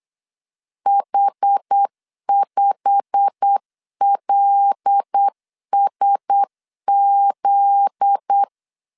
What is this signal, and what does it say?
Morse code, heard as a keyed tone, about 8 wpm, 800 Hz. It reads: H5LSZ
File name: LEZIONI / 8 YJ45 / 8 YJ45 8wpm 3214